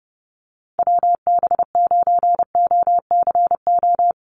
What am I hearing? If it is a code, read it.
Morse code, heard as a keyed tone, about 30 wpm, 700 Hz. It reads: WB9OCO
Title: USA Call Signs Repeated 3x 30wpm